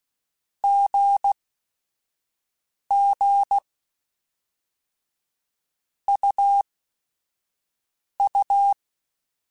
Morse code, heard as a keyed tone, about 16 wpm, spaced as for 5 wpm, 800 Hz. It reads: GG UU
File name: Basic Sets / Basic Letter Groups Collective7 / AWUNDG@5